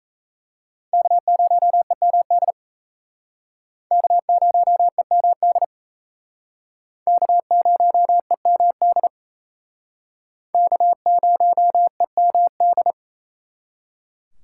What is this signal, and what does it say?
Morse code, heard as a keyed tone, about 35 wpm, 700 Hz. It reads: K0EMD K0EMD K0EMD K0EMD